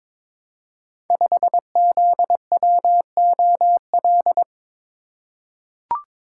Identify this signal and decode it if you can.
Morse code, heard as a keyed tone, about 22 wpm, 690 Hz. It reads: 5ZWOL
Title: Code Groups - 5 Letter:Numbers - 22wpm